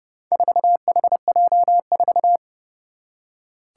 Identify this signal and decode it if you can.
Morse code, heard as a keyed tone, about 30 wpm, 700 Hz. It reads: 4HJ4
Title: Four Challenging Letter-Number Combinations 30wpm